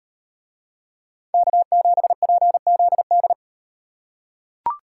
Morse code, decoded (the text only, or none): K7PZD